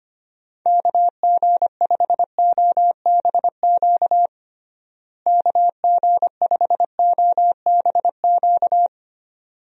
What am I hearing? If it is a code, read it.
Morse code, heard as a keyed tone, about 25 wpm, 700 Hz. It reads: KG5OBQ KG5OBQ
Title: USA Call Signs - 6 Characters - Repeated 2x - 25wpm